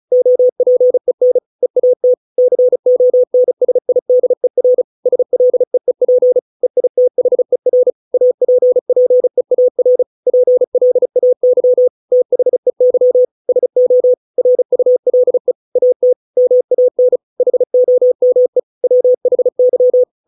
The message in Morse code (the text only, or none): OPEN EAT CONSIDER SLEEP EITHER APPEAR PLAY THEY SO RULE AT MAN HOME WHY